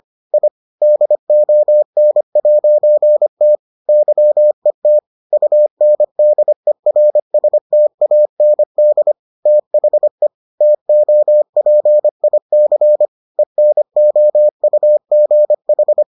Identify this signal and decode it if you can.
Morse code, heard as a keyed tone, about 25 wpm, 600 Hz. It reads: I DON'T YET UNDERSTAND THE TOPIC ENOUGH